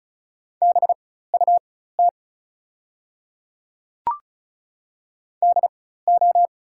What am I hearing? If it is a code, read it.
Morse code, heard as a keyed tone, about 36 wpm, spaced as for 16 wpm, 700 Hz. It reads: BUT DO
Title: 4x Character Spacing - Top 200 Words - 35wpm